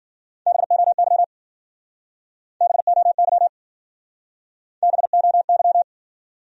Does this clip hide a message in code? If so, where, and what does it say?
Yes: Morse code from 0.5 to 5.8 s, heard as a keyed tone, about 56 wpm, 700 Hz. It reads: BOY BOY BOY